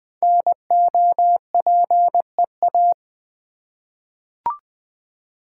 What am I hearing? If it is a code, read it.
Morse code, heard as a keyed tone, about 20 wpm, 700 Hz. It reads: NOPEA
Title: Code Groups - 5 Letters - 20wpm